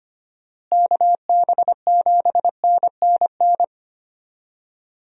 Morse code, heard as a keyed tone, about 25 wpm, 700 Hz. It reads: KB7NNN
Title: USA Call Signs - 6 Characters - Repeated 2x - 25wpm